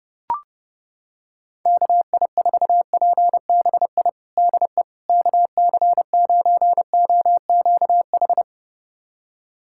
Morse code, heard as a keyed tone, about 30 wpm, 700 Hz. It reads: KI4PBI DE KC9OQH
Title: USA Callsign DE Callsign 30wpm